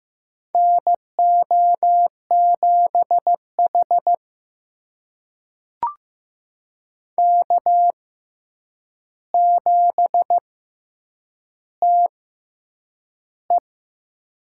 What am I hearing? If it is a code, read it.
Morse code, heard as a keyed tone, about 15 wpm, spaced as for 8 wpm, 700 Hz. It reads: NO7H K7TE
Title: USA Call Signs - 6x Character Spacing - 15wpm